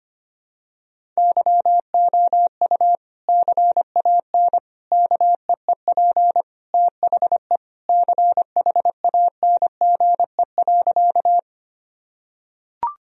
Morse code, heard as a keyed tone, about 25 wpm, 700 Hz. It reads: YOU CAN KEEP THE CHANGE.
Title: Sentences from Top 1000 Words - Review Part 1 - 25wpm